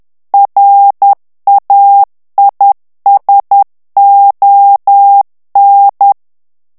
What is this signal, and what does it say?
Morse code, heard as a keyed tone, about 10 wpm, 800 Hz. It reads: RAISON
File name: CW / FICHIERS-MP3 / FRANCAIS TEXTE CLAIR 2 VITESSE 10 MOTS MINUTES